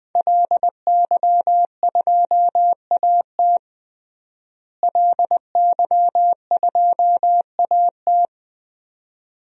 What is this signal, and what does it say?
Morse code, heard as a keyed tone, about 21 wpm, 700 Hz. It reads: LY2AT LY2AT
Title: DXCC-100 Call Signs - 5 Characters - Repeated 2x - 20wpm